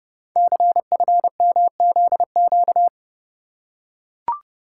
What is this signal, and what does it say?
Morse code, heard as a keyed tone, about 30 wpm, 700 Hz. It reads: CFMZQ
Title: Code Groups - 5 Letters - 30wpm